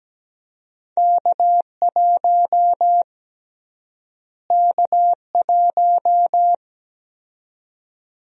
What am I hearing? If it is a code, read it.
Morse code, heard as a keyed tone, about 17 wpm, 700 Hz. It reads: K1 K1
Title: USA Call Sign Prefixes - Repeated 2x - 17wpm